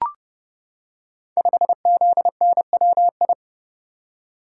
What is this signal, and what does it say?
Morse code, heard as a keyed tone, about 30 wpm, 700 Hz. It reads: 5ZNWI
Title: Code Groups - 5 Letter:Numbers - 30wpm